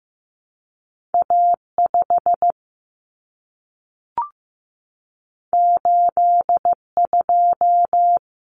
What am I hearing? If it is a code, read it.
Morse code, heard as a keyed tone, about 15 wpm, 700 Hz. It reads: A5 82
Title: Common Cut Numbers - 2-Digit 15wpm